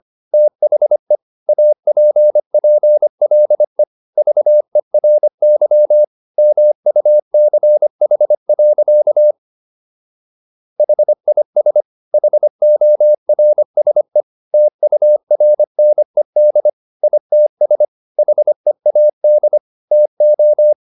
Morse code, heard as a keyed tone, about 25 wpm, 600 Hz. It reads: THE APPLE VERY MUCH. HIS HORSE TURNED ITS HEAD TO